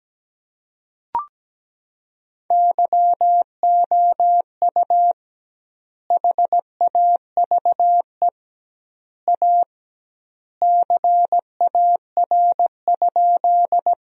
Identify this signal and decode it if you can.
Morse code, heard as a keyed tone, about 17 wpm, 700 Hz. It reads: YOU HAVE A CAR?
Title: 2x Word-Spacing - Sentences from Top 300 Words - 17wpm